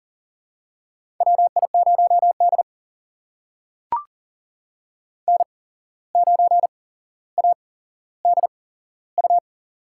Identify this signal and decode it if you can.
Morse code, heard as a keyed tone, about 40 wpm, spaced as for 16 wpm, 700 Hz. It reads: WI0D N9ADU
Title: USA Call Signs - 8x Character Spacing - 40wpm